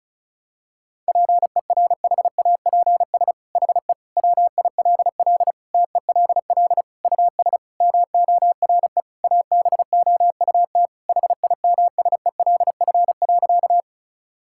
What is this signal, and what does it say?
Morse code, heard as a keyed tone, about 34 wpm, 700 Hz. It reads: PERHAPS HE WILL TELL US MORE ABOUT HIMSELF.
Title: Sentences from Top 1600 Words 35wpm